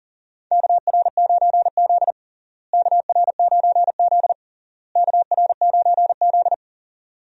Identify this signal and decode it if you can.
Morse code, heard as a keyed tone, about 40 wpm, 700 Hz. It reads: KR9Z KR9Z KR9Z
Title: USA Call Signs - 4 Characters - Repeated 3x - 40wpm